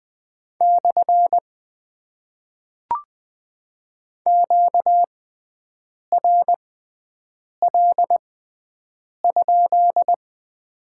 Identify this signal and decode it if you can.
Morse code, heard as a keyed tone, about 20 wpm, spaced as for 7 wpm, 700 Hz. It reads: / QRL?
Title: Top 90 QSO Elements - 6x Character Spacing - 20wpm